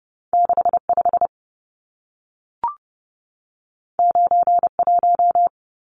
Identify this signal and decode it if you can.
Morse code, heard as a keyed tone, about 30 wpm, 700 Hz. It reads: 65 91
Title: Numbers - 2-Digit 30wpm